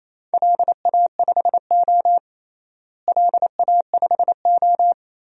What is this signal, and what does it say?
Morse code, heard as a keyed tone, about 28 wpm, 700 Hz. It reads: LA5O LA5O
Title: DXCC-100 Call Signs - 4 Characters - Repeated 3x - 28wpm